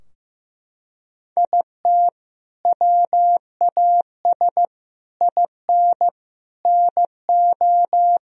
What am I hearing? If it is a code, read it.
Morse code, heard as a keyed tone, about 15 wpm, 700 Hz. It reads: IT WAS IN NO
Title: Sentences from Top 1000 Words - Review Part 4 - 15wpm